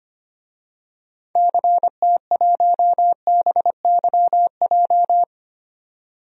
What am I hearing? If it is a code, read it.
Morse code, heard as a keyed tone, about 25 wpm, 700 Hz. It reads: CT1BYJ
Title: DXCC-100 Call Signs - 6 Characters - 25wpm